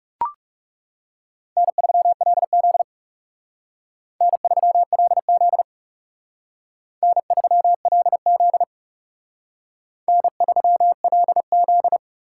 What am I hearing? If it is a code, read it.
Morse code, heard as a keyed tone, about 36 wpm, 700 Hz. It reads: N3LZ N3LZ N3LZ N3LZ
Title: Speed Racing - USA Call Signs 30wpm